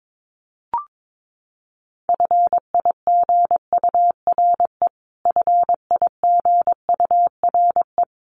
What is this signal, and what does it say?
Morse code, heard as a keyed tone, about 22 wpm, 700 Hz. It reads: FIGURE FIGURE
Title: Top 500 Words Repeated 22wpm